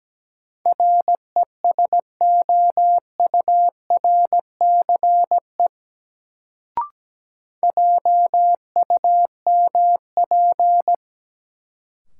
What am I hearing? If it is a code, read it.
Morse code, heard as a keyed tone, about 17 wpm, 700 Hz. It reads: RESOURCE JUMP
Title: Sentences from Top 1300 Words - New Word Drill - 17wpm